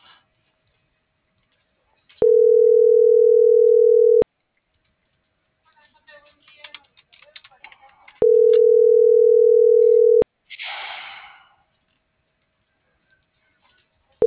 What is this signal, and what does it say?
Ringback tone (North America), 440 + 480 Hz.